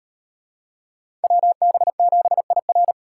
Morse code, heard as a keyed tone, about 36 wpm, 700 Hz. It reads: WB7IR